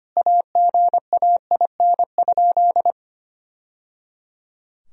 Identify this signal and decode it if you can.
Morse code, heard as a keyed tone, about 25 wpm, 700 Hz. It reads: AGAIN?
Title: Sentences from Top 1000 Words - Review Part 4 - 25wpm